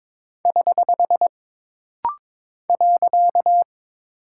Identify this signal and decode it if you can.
Morse code, heard as a keyed tone, about 22 wpm, 700 Hz. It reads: <HH> .
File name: practice / 22wpm / Single Punctuation-Prosign - Rapid-Fire - 22wpm